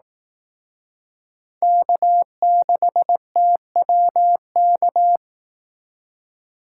Morse code, heard as a keyed tone, about 18 wpm, 700 Hz. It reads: K6TWK
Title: Speed Racing - USA 1x3 Call Signs 15wpm